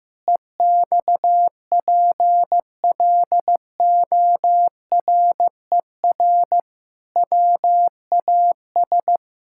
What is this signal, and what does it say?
Morse code, heard as a keyed tone, about 15 wpm, 700 Hz. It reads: EXPLORER WAS
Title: Pangram Challenge v2 - 15wpm